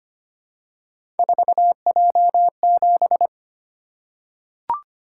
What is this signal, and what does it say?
Morse code, heard as a keyed tone, about 25 wpm, 700 Hz. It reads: 4J7
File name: Three Letter-Number Combinations - 25wpm